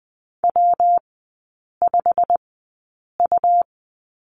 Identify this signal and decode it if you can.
Morse code, heard as a keyed tone, about 20 wpm, spaced as for 8 wpm, 700 Hz. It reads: W5U
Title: 127 Lesson 21e - Number 6 - Three Characters 20wpm